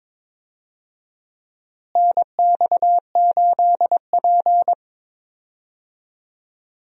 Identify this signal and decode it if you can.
Morse code, heard as a keyed tone, about 22 wpm, 700 Hz. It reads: NX8P
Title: USA Call Signs - 4 Characters - Repeated 2x - 22wpm